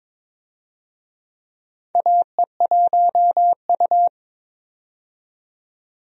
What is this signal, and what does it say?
Morse code, heard as a keyed tone, about 22 wpm, 700 Hz. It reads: AE1U
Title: USA Call Signs - 4 Characters - 22wpm